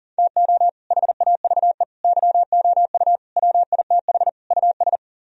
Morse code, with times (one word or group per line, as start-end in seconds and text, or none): 0.2-0.7 TO
0.9-1.8 HAVE
2.0-3.2 YOU
3.4-4.3 WITH
4.5-5.0 US